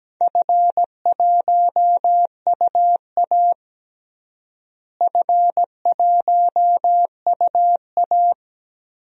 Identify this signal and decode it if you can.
Morse code, heard as a keyed tone, about 17 wpm, 700 Hz. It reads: F1UA F1UA
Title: DXCC-100 Call Signs - 4 Characters - Repeated 2x - 17wpm